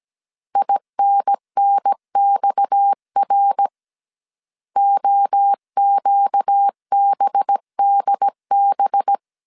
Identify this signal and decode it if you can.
Morse code, heard as a keyed tone, about 17 wpm, 800 Hz. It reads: INNXR OQBDB
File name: 16wpm